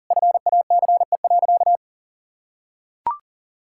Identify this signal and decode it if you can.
Morse code, heard as a keyed tone, about 40 wpm, 700 Hz. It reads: FACE.
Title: Sentences from Top 1000 Words - Review Part 5 - 40wpm